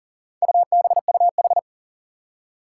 Morse code, heard as a keyed tone, about 40 wpm, 700 Hz. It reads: UBUH
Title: Four Challenging Letter-Number Combinations 40wpm